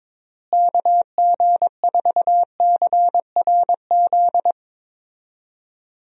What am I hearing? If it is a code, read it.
Morse code, heard as a keyed tone, about 22 wpm, 700 Hz. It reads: KG4CRZ
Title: USA Call Signs - 6 Characters - Repeated 2x - 22wpm